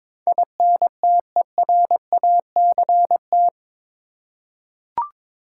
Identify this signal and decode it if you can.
Morse code, heard as a keyed tone, about 22 wpm, 700 Hz. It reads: INTERACT
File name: practice / 22wpm / Eight Letter Words - 22wpm